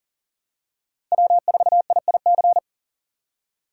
Morse code, heard as a keyed tone, about 40 wpm, 700 Hz. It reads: W4IIC
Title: K1USN SST QSO Practice - 40wpm